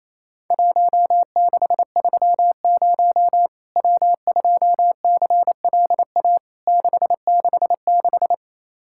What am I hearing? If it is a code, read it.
Morse code, heard as a keyed tone, about 28 wpm, 700 Hz. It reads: 1630 W2CLA 666
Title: Worked All Europe DX Contest - QTC Exchange - 28wpm